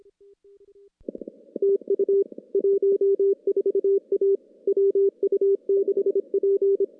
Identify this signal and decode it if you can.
Morse code, heard as a keyed tone, about 25 wpm, 400 Hz. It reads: TU 14A WU6P
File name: contests / CW-SS-k5tr- -0000